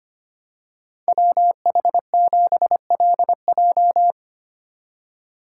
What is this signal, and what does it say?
Morse code, heard as a keyed tone, about 27 wpm, 700 Hz. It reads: WH7LJ
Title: Speed Racing - USA 2x2 Call Signs 25wpm